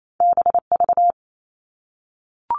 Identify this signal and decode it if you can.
Morse code, heard as a keyed tone, about 28 wpm, 700 Hz. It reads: BV
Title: All Cut Numbers - 2-Digit 28wpm